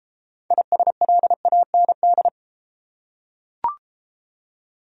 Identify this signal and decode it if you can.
Morse code, heard as a keyed tone, about 34 wpm, 700 Hz. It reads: ISLAND